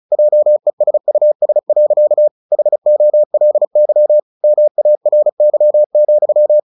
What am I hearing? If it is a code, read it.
Morse code, heard as a keyed tone, about 34 wpm, 600 Hz. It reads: JESUS. HOLY MARY,
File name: the old man and the sea / the old man and the sea 1 35wpm